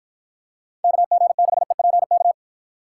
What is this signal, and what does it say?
Morse code, heard as a keyed tone, about 52 wpm, 700 Hz. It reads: KG6EPK